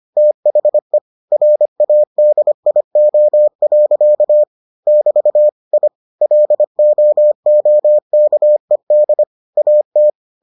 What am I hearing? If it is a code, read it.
Morse code, heard as a keyed tone, about 25 wpm, 600 Hz. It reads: THE RADIO. <BT> I LOOKED AT